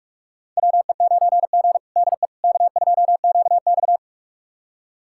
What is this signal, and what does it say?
Morse code, heard as a keyed tone, about 46 wpm, 700 Hz. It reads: WE9G DE K2QX
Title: USA Callsign DE Callsign - 4 Characters - 45wpm